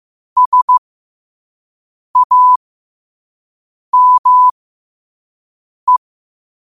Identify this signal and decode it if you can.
Morse code, heard as a keyed tone, about 15 wpm, spaced as for 5 wpm, 1000 Hz. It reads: SAME
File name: qotdfiles / 5wpm